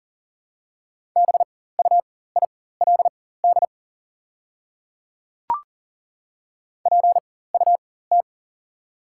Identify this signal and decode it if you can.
Morse code, heard as a keyed tone, about 40 wpm, spaced as for 19 wpm, 700 Hz. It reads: BUILD PUT
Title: 4x Character Spacing - Top 200 Words - 40wpm